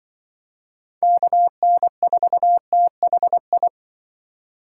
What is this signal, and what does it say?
Morse code, heard as a keyed tone, about 25 wpm, 700 Hz. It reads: KN4THI
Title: Speed Racing - USA 2x3 Call Signs 20wpm